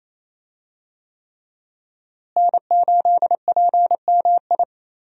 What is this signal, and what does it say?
Morse code, heard as a keyed tone, about 27 wpm, 700 Hz. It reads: N8PMI
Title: USA Call Signs - 5 Characters - Repeated 3x - 28wpm